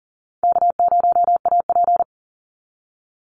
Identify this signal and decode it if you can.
Morse code, heard as a keyed tone, about 40 wpm, 700 Hz. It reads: K0AP
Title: Easy Call Signs Repeated 3x - 40wpm